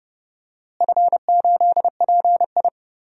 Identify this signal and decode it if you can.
Morse code, heard as a keyed tone, about 30 wpm, 700 Hz. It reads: F8PI
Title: DXCC-100 Call Signs - 4 Characters - Repeated 2x - 30wpm